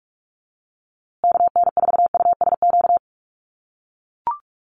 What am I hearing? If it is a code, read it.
Morse code, heard as a keyed tone, about 44 wpm, 700 Hz. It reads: KN4USQ
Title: USA Call Signs Repeated 3x 45wpm